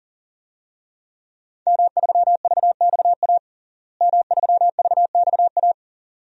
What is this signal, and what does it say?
Morse code, heard as a keyed tone, about 40 wpm, 700 Hz. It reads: M3VXA M3VXA
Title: DXCC-100 Call Signs - 5 Characters - Repeated 2x - 40wpm